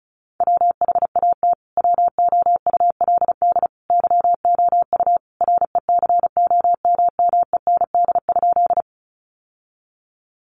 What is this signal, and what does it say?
Morse code, heard as a keyed tone, about 34 wpm, 700 Hz. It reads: WHAT WOULD YOU RECOMMEND?
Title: Sentences from Top 1200 Words 35wpm